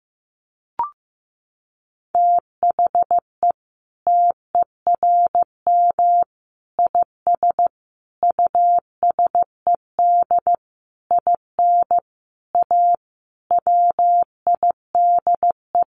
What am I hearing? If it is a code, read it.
Morse code, heard as a keyed tone, about 15 wpm, 700 Hz. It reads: THE TERM IS USED IN A WIDE
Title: Sentences from Top 900 Words 15wpm